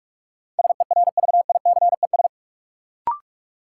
Morse code, heard as a keyed tone, about 46 wpm, 700 Hz. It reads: SERVICES
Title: Sets of 5 Words - New Word Drill - 45wpm